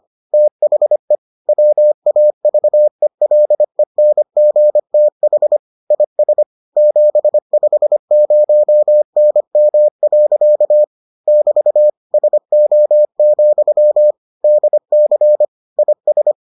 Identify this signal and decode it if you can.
Morse code, heard as a keyed tone, about 25 wpm, 600 Hz. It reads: THE WAVELENGTH IS 750NM. <BT> SO, DC IS